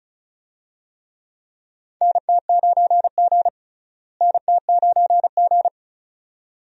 Morse code, heard as a keyed tone, about 36 wpm, 700 Hz. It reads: NT9G NT9G
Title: USA Call Signs - 4 Characters - Repeated 2x - 35wpm